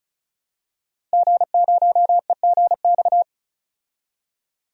Morse code, heard as a keyed tone, about 34 wpm, 700 Hz. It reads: G0EGX
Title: DXCC-100 Call Signs - 5 Characters - 35wpm